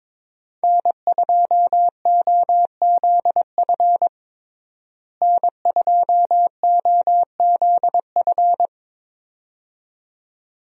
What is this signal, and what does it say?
Morse code, heard as a keyed tone, about 22 wpm, 700 Hz. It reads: N2OZF N2OZF
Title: USA Call Signs - 5 Characters - Repeated 2x - 22wpm